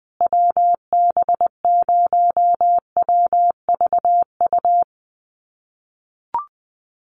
Morse code, heard as a keyed tone, about 19 wpm, 700 Hz. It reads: WB0WVU
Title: USA Call Signs Repeated 2x 20wpm